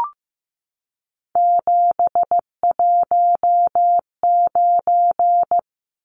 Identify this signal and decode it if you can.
Morse code, heard as a keyed tone, about 15 wpm, 700 Hz. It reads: 719
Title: Numbers - 3-Digit 15wpm